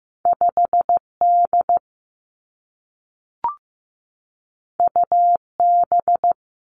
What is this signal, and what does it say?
Morse code, heard as a keyed tone, about 15 wpm, 700 Hz. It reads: 5 D U B
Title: Two Challenging Letter-Number Combinations 15wpm